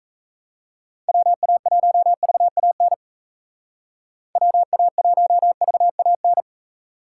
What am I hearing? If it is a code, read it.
Morse code, heard as a keyed tone, about 35 wpm, 700 Hz. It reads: WA1VAN WA1VAN
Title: Speed Racing - USA 2x3 Call Signs 28wpm